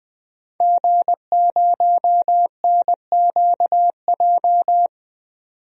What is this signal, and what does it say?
Morse code, heard as a keyed tone, about 21 wpm, 700 Hz. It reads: G0NQJ
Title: DXCC-100 Call Signs - 5 Characters - 20wpm